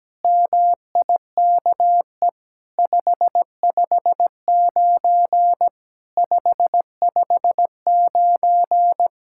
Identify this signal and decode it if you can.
Morse code, heard as a keyed tone, about 17 wpm, 700 Hz. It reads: MIKE 559 559
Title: SOTA QSO Practice 17wpm